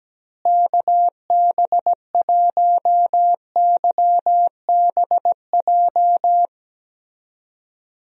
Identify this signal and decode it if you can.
Morse code, heard as a keyed tone, about 17 wpm, 700 Hz. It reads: KB1YBJ
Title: USA Call Signs - 6 Characters - Repeated 2x - 17wpm